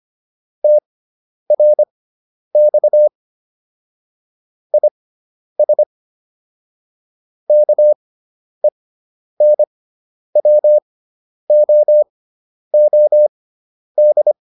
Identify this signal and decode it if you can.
Morse code, heard as a keyed tone, about 25 wpm, spaced as for 10 wpm, 600 Hz. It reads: TRX IS KENWOOD
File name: QSO201 10